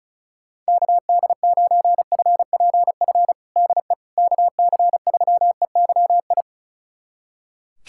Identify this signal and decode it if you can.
Morse code, heard as a keyed tone, about 36 wpm, 700 Hz. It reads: KD9FPF DE KC3EYI
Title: USA Callsign DE Callsign - 6 Characters - 35wpm